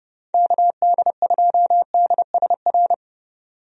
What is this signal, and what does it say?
Morse code, heard as a keyed tone, about 32 wpm, 700 Hz. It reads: KD2DSR
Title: USA Call Signs - 6 Characters - 30wpm